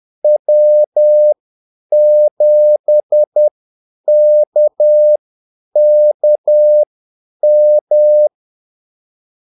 Morse code, heard as a keyed tone, about 10 wpm, spaced as for 8 wpm, 600 Hz. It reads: W7KKM
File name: Callsigns-10-8wpm